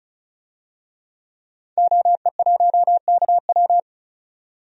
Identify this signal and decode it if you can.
Morse code, heard as a keyed tone, about 36 wpm, 700 Hz. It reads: OE1KW
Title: DXCC-100 Call Signs - 5 Characters - 35wpm